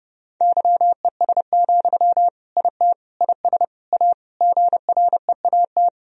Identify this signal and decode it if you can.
Morse code, heard as a keyed tone, about 30 wpm, 700 Hz. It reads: YES, IT IS A GREAT
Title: Sentences from Top 1000 Words - Review Part 2 - 30wpm